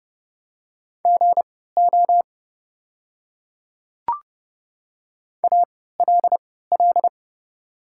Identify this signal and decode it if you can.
Morse code, heard as a keyed tone, about 30 wpm, spaced as for 17 wpm, 700 Hz. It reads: GO ALL
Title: 3x Character Spacing - Top 100 Words - 30wpm